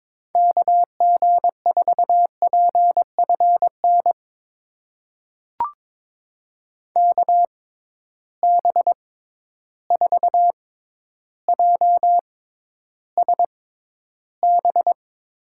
Morse code, heard as a keyed tone, about 22 wpm, spaced as for 11 wpm, 700 Hz. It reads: KG4PFN KB4JSB
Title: USA Call Signs - 6 Characters - 6x Character Spacing - 22wpm